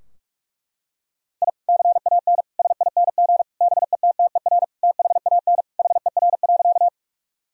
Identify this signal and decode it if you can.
Morse code, heard as a keyed tone, about 46 wpm, 700 Hz. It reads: I CAN SING BETTER THAN HER.
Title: Sentences from Top 1000 Words - Review Part 2 - 45wpm